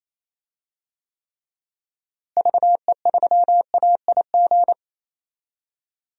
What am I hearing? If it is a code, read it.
Morse code, heard as a keyed tone, about 28 wpm, 700 Hz. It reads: VE3AIG
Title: DXCC-100 Call Signs - 6 Characters - Repeated 3x - 28wpm